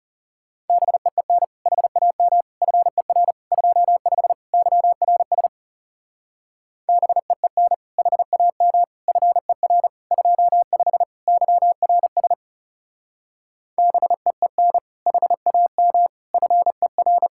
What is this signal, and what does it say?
Morse code, heard as a keyed tone, about 36 wpm, 700 Hz. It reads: BEEN HAM FER 25 YRS BEEN HAM FER 25 YRS BEEN HAM FER
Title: Speed-Racing Common QSO Phrases - 30wpm